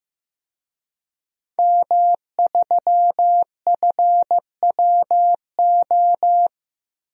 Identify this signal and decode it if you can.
Morse code, heard as a keyed tone, about 15 wpm, 700 Hz. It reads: M3FWO
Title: DXCC-100 Call Signs - 5 Characters - Repeated 3x - 15wpm